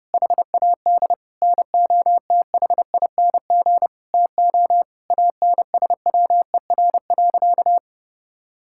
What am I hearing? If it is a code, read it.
Morse code, heard as a keyed tone, about 30 wpm, 700 Hz. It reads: HAD NOTHING TO ANSWER.
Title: Sentences from Top 1600 Words 30wpm